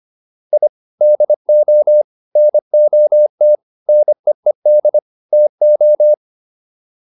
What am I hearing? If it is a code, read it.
Morse code, heard as a keyed tone, about 25 wpm, 600 Hz. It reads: I DO NOT NEED TO